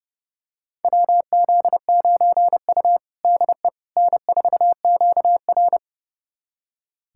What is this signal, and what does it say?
Morse code, heard as a keyed tone, about 30 wpm, 700 Hz. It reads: WZ9U DE N4QR
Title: USA Callsign DE Callsign - 4 Characters - 30wpm